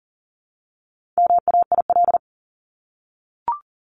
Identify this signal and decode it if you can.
Morse code, heard as a keyed tone, about 40 wpm, 700 Hz. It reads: MAIL
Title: Top 1751-2000 Words 40wpm